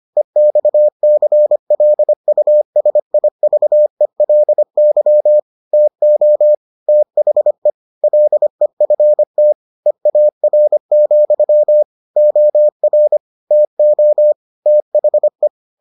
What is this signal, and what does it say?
Morse code, heard as a keyed tone, about 25 wpm, 600 Hz. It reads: EXCLUSIVELY TO THE LEFT EAR, OR TO THE